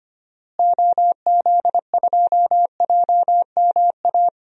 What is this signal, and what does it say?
Morse code, heard as a keyed tone, about 25 wpm, 700 Hz. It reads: OZ2JMA
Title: DXCC-100 Call Signs - 6 Characters - Repeated 3x - 25wpm